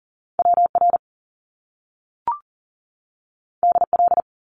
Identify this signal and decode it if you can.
Morse code, heard as a keyed tone, about 40 wpm, 700 Hz. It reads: WR DL
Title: Top Two Letter Combinations in Words 40wpm